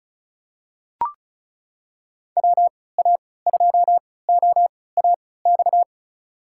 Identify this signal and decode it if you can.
Morse code, heard as a keyed tone, about 36 wpm, spaced as for 20 wpm, 700 Hz. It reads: WA2OAX
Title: USA Call Signs - 6 Characters - 3x Character Spacing - 35wpm